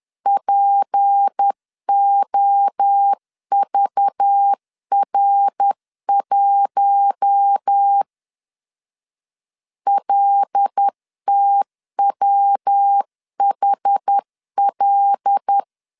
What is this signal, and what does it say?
Morse code, heard as a keyed tone, about 10 wpm, 800 Hz. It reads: POVR1 LTWHL
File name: CW / DET / 1 6 1500C S C 10wpm 4252